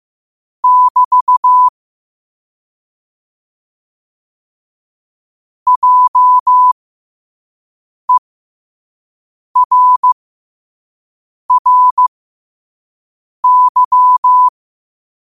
Morse code, heard as a keyed tone, about 15 wpm, spaced as for 5 wpm, 1000 Hz. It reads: <BT> JERRY
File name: qotdfiles / 5wpm